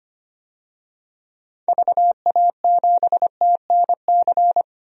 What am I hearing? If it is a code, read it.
Morse code, heard as a keyed tone, about 25 wpm, 700 Hz. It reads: VA7TNC
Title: DXCC-100 Call Signs - 6 Characters - Repeated 3x - 25wpm